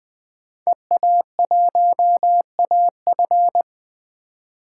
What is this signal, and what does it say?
Morse code, heard as a keyed tone, about 20 wpm, 700 Hz. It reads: EA1AF
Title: 2023 POTA Hunter Call Signs - 20wpm